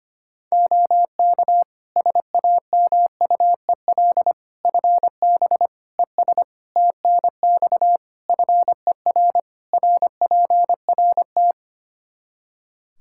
Morse code, heard as a keyed tone, about 25 wpm, 700 Hz. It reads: OK SAMUEL FB ES TNX FER RPRT
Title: K1USN SST QSO Practice 25wpm